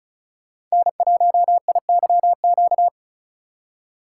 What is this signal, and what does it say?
Morse code, heard as a keyed tone, about 36 wpm, 700 Hz. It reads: N1IYQ
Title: USA Call Signs - 4x Character Spacing - 35wpm